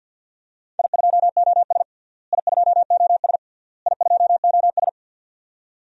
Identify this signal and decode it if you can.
Morse code, heard as a keyed tone, about 53 wpm, 700 Hz. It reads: I2OS I2OS I2OS
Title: DXCC-100 Call Signs - 4 Characters - Repeated 3x - 50wpm